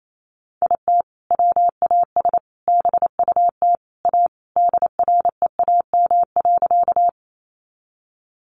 Morse code, heard as a keyed tone, about 28 wpm, 700 Hz. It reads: IT WAS BUT A DREAM.
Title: Sentences from Top 1500 Words 28wpm